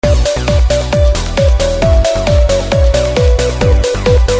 There is no code.